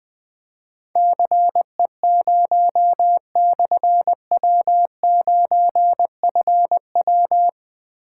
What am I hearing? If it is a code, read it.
Morse code, heard as a keyed tone, about 20 wpm, 700 Hz. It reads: CE0/W9FW
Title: Difficult Call Signs 20wpm